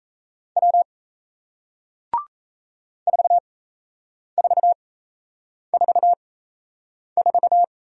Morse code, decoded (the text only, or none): W 4444